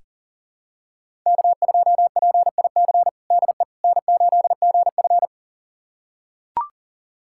Morse code, heard as a keyed tone, about 40 wpm, 700 Hz. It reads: K2PIC DE N8GF
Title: USA Callsign DE Callsign 40wpm